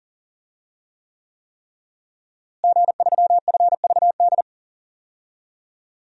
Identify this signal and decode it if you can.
Morse code, heard as a keyed tone, about 40 wpm, 700 Hz. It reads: G3FVD
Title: DXCC-100 Call Signs - 5 Characters - Repeated 3x - 40wpm